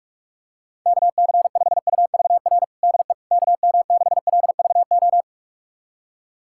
Morse code, heard as a keyed tone, about 42 wpm, 700 Hz. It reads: KC5UVR DE KM6LVO
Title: USA Callsign DE Callsign 45wpm